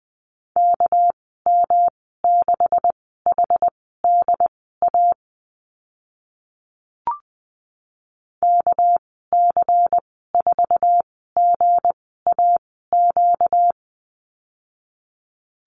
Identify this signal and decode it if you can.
Morse code, heard as a keyed tone, about 20 wpm, spaced as for 14 wpm, 700 Hz. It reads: KM6HDA KC4GAQ
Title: 245 Lesson 39g - Letter z - Call Signs 20wpm